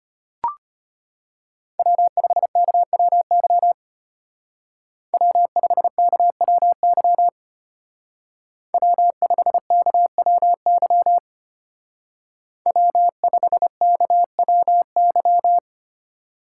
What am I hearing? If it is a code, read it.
Morse code, heard as a keyed tone, about 30 wpm, 700 Hz. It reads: W5KWY W5KWY W5KWY W5KWY
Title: Speed Racing - USA Call Signs 25wpm